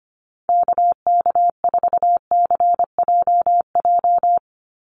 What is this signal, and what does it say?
Morse code, heard as a keyed tone, about 25 wpm, 700 Hz. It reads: KK4CJJ
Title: USA Call Signs 25wpm